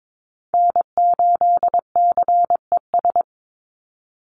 Morse code, heard as a keyed tone, about 22 wpm, 700 Hz. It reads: N8CES
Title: USA Call Signs 22wpm